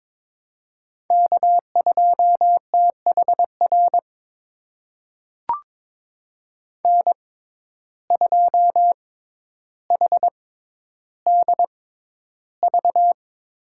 Morse code, heard as a keyed tone, about 22 wpm, spaced as for 11 wpm, 700 Hz. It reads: K2THR N2HDV
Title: USA Call Signs - 5 Characters - 6x Character Spacing - 22wpm